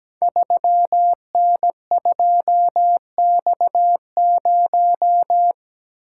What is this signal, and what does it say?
Morse code, heard as a keyed tone, about 17 wpm, 700 Hz. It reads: 3N2X0
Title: Code Groups - 5 Letter:Numbers - 17wpm